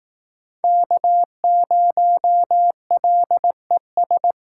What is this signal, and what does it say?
Morse code, heard as a keyed tone, about 18 wpm, 700 Hz. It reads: K0LES
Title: Speed Racing - USA Call Signs 15wpm